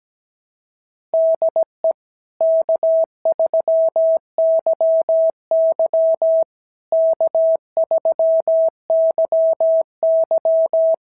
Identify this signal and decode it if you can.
Morse code, heard as a keyed tone, about 17 wpm, 650 Hz. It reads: DE K3YY K3YY
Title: SOTA QSO Practice 17wpm